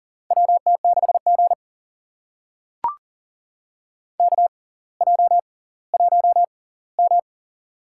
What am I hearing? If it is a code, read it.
Morse code, heard as a keyed tone, about 40 wpm, spaced as for 20 wpm, 700 Hz. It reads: WT6G KJ1M
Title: USA Call Signs - 4 Characters - 6x Character Spacing - 40wpm